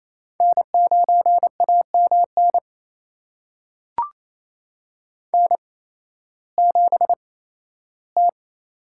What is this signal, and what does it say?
Morse code, heard as a keyed tone, about 28 wpm, spaced as for 15 wpm, 700 Hz. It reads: N9AMN N7T
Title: USA Call Signs - 5 Characters - 8x Character Spacing - 28wpm